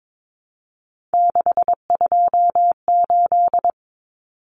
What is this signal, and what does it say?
Morse code, heard as a keyed tone, about 22 wpm, 700 Hz. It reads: 628
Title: Numbers - 3-Digit 22wpm